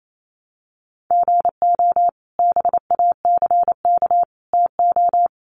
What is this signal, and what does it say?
Morse code, heard as a keyed tone, about 28 wpm, 700 Hz. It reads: GO BACK TO
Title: Sets of 4 Words 28wpm